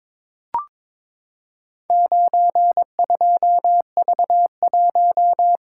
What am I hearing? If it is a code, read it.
Morse code, heard as a keyed tone, about 23 wpm, 700 Hz. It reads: 92V1
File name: practice / 22wpm / Four Letter-Number Combinations - 22wpm